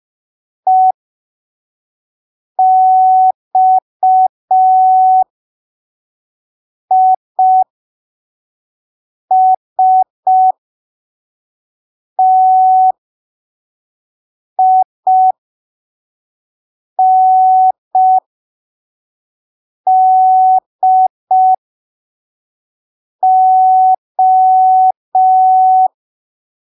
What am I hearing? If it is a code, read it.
Morse code, heard as a keyed tone, about 5 wpm, spaced as for 3 wpm, 750 Hz. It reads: EXISTINDO
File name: arquivos / texto / texto 04 5ppm